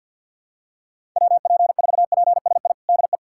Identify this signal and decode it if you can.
Morse code, heard as a keyed tone, about 46 wpm, 700 Hz. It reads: WP4PSI DE
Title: USA Callsign DE Callsign - 6 Characters - 50wpm